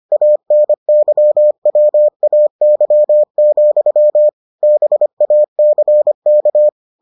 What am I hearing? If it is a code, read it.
Morse code, heard as a keyed tone, about 25 wpm, 600 Hz. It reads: ANYWAY, BACK